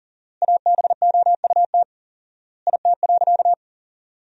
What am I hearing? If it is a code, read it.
Morse code, heard as a keyed tone, about 40 wpm, 700 Hz. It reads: ABOUT IT.
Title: 4x Word-Spacing - Sentences from Top 500 Words - 40wpm